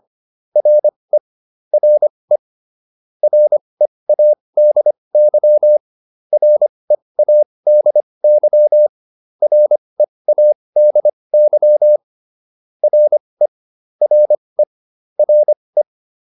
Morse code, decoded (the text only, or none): RE RE READY READY READY RE RE RE